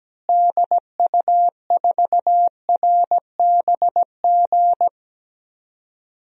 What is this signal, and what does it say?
Morse code, heard as a keyed tone, about 17 wpm, 700 Hz. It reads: DU4RBG
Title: DXCC-100 Call Signs - 6 Characters - Repeated 3x - 17wpm